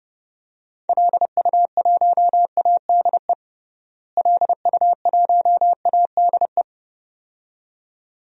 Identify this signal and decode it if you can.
Morse code, heard as a keyed tone, about 30 wpm, 700 Hz. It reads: LU1ADE LU1ADE
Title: DXCC-100 Call Signs - 6 Characters - Repeated 2x - 30wpm